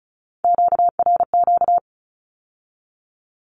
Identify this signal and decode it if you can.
Morse code, heard as a keyed tone, about 36 wpm, 700 Hz. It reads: QRQ
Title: Q-Codes with Spoken Meaning - 35wpm